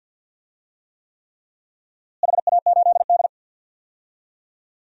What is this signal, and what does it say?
Morse code, heard as a keyed tone, about 51 wpm, 700 Hz. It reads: HA8D
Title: DXCC-100 Call Signs - 4 Characters - Repeated 3x - 50wpm